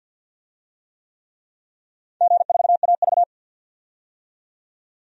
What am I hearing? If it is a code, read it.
Morse code, heard as a keyed tone, about 52 wpm, 700 Hz. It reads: G4AV
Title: DXCC-100 Call Signs - 4 Characters - 50wpm